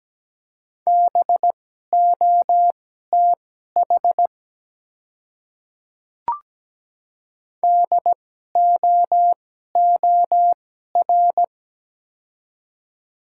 Morse code, heard as a keyed tone, about 17 wpm, spaced as for 12 wpm, 690 Hz. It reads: BOTH DOOR
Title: 2x Character Spacing - Top 300 Words - 17wpm